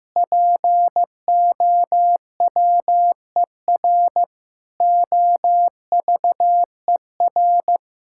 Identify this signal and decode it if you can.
Morse code, heard as a keyed tone, about 16 wpm, 700 Hz. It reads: POWER OVER
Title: Sentences from Top 1000 Words - Review Part 5 - 15wpm